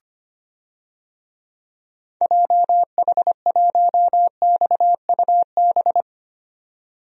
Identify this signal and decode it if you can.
Morse code, heard as a keyed tone, about 25 wpm, 700 Hz. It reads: JH1XUB
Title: DXCC-100 Call Signs - 6 Characters - 25wpm